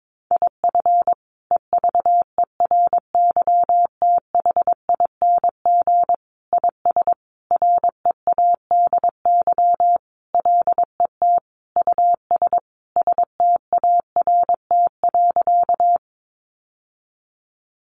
Morse code, heard as a keyed tone, about 22 wpm, 700 Hz. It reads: IF EVERYTHING IS READY LET US START.
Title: Sentences from Top 700 Words 22wpm